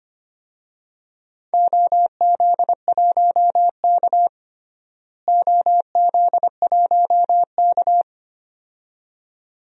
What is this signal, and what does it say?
Morse code, heard as a keyed tone, about 25 wpm, 700 Hz. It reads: OZ1K OZ1K
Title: DXCC-100 Call Signs - 4 Characters - Repeated 2x - 25wpm